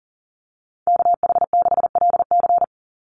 Morse code, heard as a keyed tone, about 40 wpm, 700 Hz. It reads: KH6LC
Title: Call Signs Repeated 3x - 40wpm